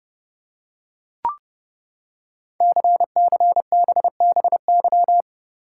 Morse code, heard as a keyed tone, about 29 wpm, 700 Hz. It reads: CCBBY
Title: Code Groups - 5 Letters - 30wpm